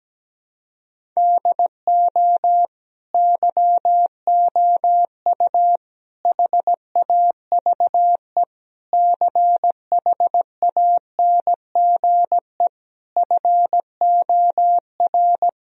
Morse code, heard as a keyed tone, about 17 wpm, 700 Hz. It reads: DO YOU HAVE CHANGE FOR
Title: Sentences from Top 1000 Words - Review Part 2 - 17wpm